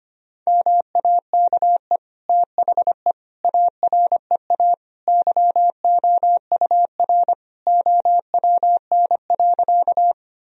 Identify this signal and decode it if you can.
Morse code, heard as a keyed tone, about 25 wpm, 700 Hz. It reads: MAKE THE AREA YOUR OWN.
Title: Sentences from Top 1000 Words - Review Part 2 - 25wpm